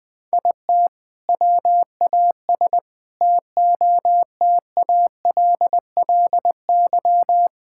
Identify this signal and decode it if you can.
Morse code, heard as a keyed tone, about 21 wpm, 700 Hz. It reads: IT WAS TOTALLY
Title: Sentences from Top 1600 Words 20wpm